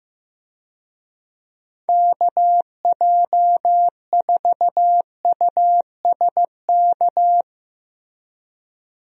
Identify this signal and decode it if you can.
Morse code, heard as a keyed tone, about 15 wpm, 700 Hz. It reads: KJ4USK
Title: USA Call Signs - 6 Characters - 15wpm